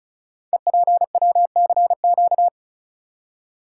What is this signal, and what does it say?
Morse code, heard as a keyed tone, about 36 wpm, 690 Hz. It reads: EPWCQ
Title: Code Groups - 5 Letter:Numbers - 35wpm